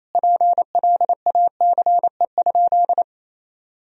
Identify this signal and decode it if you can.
Morse code, heard as a keyed tone, about 28 wpm, 700 Hz. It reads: PLACE?
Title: Sentences from Top 1000 Words - Review Part 2 - 28wpm